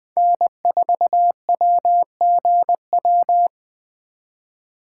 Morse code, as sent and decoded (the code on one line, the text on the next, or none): -. ....- .-- --. .--
N4WGW